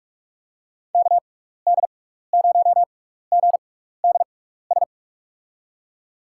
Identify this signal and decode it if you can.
Morse code, heard as a keyed tone, about 46 wpm, spaced as for 15 wpm, 700 Hz. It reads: KD0GDS
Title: USA Call Signs - 6 Characters - 6x Character Spacing - 45wpm